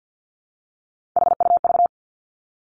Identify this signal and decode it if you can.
Morse code, heard as a keyed tone, about 50 wpm, 700 Hz. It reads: HUV
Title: Three Challenging Letter-Number Combinations 50wpm